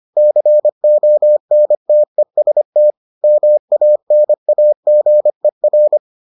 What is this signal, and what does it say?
Morse code, heard as a keyed tone, about 25 wpm, 600 Hz. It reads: CONTEST MANAGER